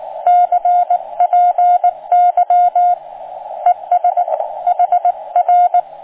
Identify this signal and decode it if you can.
Morse code, heard as a keyed tone, about 18 wpm, 710 Hz, with radio noise. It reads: CPY ES HR